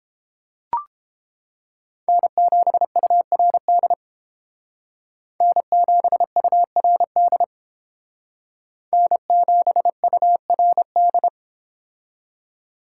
Morse code, heard as a keyed tone, about 28 wpm, 700 Hz. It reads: N7URD N7URD N7URD